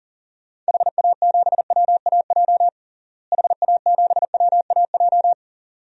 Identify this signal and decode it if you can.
Morse code, heard as a keyed tone, about 40 wpm, 700 Hz. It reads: HA7WAJ HA7WAJ
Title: DXCC-100 Call Signs - 6 Characters - Repeated 2x - 40wpm